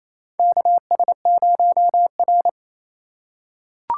Morse code, heard as a keyed tone, about 28 wpm, 700 Hz. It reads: KS0R
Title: USA Call Signs - 4 Characters - 4x Character Spacing - 28wpm